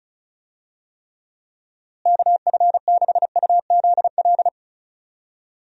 Morse code, heard as a keyed tone, about 36 wpm, 700 Hz. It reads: KF6UZL
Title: USA Call Signs - 6 Characters - Repeated 2x - 35wpm